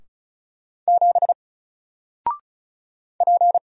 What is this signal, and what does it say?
Morse code, heard as a keyed tone, about 36 wpm, 700 Hz. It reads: 7 P
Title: Single Letter-Number Alt 35wpm